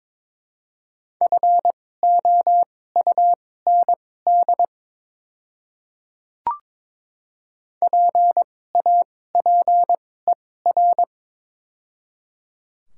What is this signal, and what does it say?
Morse code, heard as a keyed tone, about 22 wpm, spaced as for 16 wpm, 700 Hz. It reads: FOUND PAPER
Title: 2x Character Spacing - Top 500 Words - 22wpm